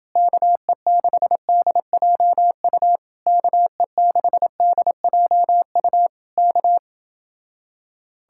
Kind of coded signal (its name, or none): Morse code